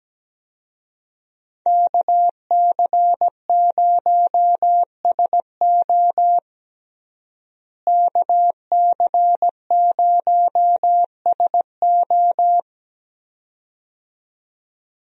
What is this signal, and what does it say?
Morse code, heard as a keyed tone, about 17 wpm, 700 Hz. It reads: KC0SO KC0SO
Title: USA Call Signs - 5 Characters - Repeated 2x - 17wpm